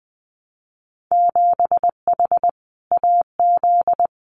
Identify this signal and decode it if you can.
Morse code, heard as a keyed tone, about 20 wpm, 700 Hz. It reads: 7H AZ
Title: Winter Field Day Exchanges Encore 20wpm